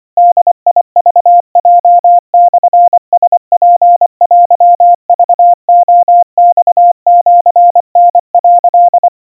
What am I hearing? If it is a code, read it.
Morse code (Japanese Wabun code), heard as a keyed tone, about 24 wpm, 700 Hz. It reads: ボクヲモラツテクレマシタ」
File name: 90 cpm 700 Hz Crispy